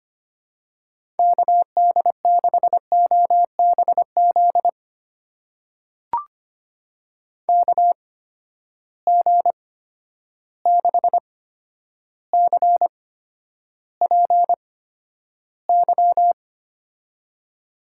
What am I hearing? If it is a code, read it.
Morse code, heard as a keyed tone, about 25 wpm, spaced as for 11 wpm, 700 Hz. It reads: KD6OBZ KG6CPY
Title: USA Call Signs - 6 Characters - 8x Character Spacing - 25wpm